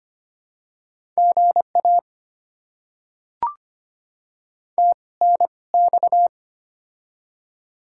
Morse code, heard as a keyed tone, about 25 wpm, spaced as for 20 wpm, 700 Hz. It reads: GA TNX
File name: Top 20 QSO Elements - 2x Character Spacing - 25wpm